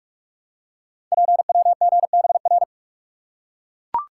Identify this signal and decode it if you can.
Morse code, heard as a keyed tone, about 44 wpm, 690 Hz. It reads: PWGBR
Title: Code Groups - 5 Letters - 45wpm